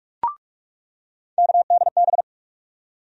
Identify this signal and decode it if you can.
Morse code, heard as a keyed tone, about 43 wpm, 700 Hz. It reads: KDB
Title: USA Call Sign Postfixes - 3 Letters 45wpm